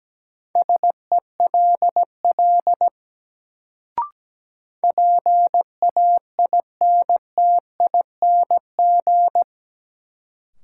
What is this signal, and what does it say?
Morse code, heard as a keyed tone, about 17 wpm, 700 Hz. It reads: SELL PAINTING
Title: Sentences from Top 1000 Words - Review Part 6 - New Word Drill - 17wpm